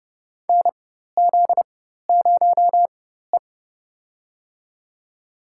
Morse code, heard as a keyed tone, about 31 wpm, spaced as for 14 wpm, 700 Hz. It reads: NZ0E
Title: USA Call Signs - 4x Character Spacing - 30wpm